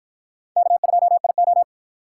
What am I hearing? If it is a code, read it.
Morse code, heard as a keyed tone, about 53 wpm, 700 Hz. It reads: K2IO